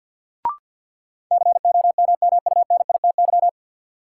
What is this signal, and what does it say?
Morse code, heard as a keyed tone, about 52 wpm, 700 Hz. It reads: COMMUNITY
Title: Sentences from Top 1000 Words - Review Part 4 - New Word Drill - 50wpm